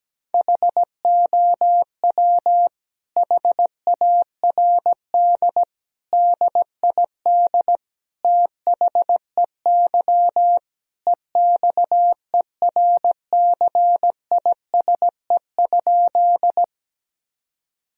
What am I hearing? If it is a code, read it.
Morse code, heard as a keyed tone, about 17 wpm, 700 Hz. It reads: HOW HARD DID THEY EXERCISE?
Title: Sentences from Top 1000 Words - Review Part 4 - 17wpm